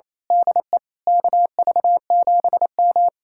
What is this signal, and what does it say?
Morse code, heard as a keyed tone, about 26 wpm, 700 Hz. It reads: DE KV7M
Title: Speed-Racing Common QSO Phrases Encore - 28wpm